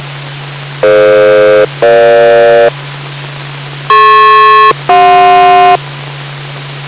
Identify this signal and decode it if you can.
Aircraft SELCAL, heard as two pairs of tones. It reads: DF-EG DP-CJ